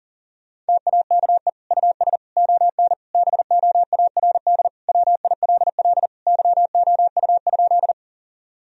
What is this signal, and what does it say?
Morse code, heard as a keyed tone, about 40 wpm, 700 Hz. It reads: TAKE US ON BOARD WILL YOU?
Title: Sentences from Top 1000 Words - Review Part 4 - 40wpm